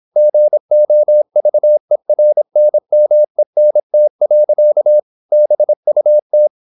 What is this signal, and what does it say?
Morse code, heard as a keyed tone, about 26 wpm, 600 Hz. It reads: GOVERNMENT. BUT
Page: Never 26wpm | CW med Gnister